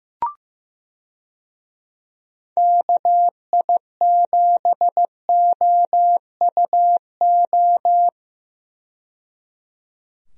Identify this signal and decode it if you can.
Morse code, heard as a keyed tone, about 15 wpm, 700 Hz. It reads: KI7OUO
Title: USA Call Signs - 6 Characters - 15wpm